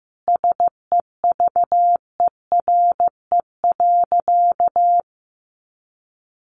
Morse code, heard as a keyed tone, about 15 wpm, 700 Hz. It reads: SEVERE.
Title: Sentences from Top 1400 Words 15wpm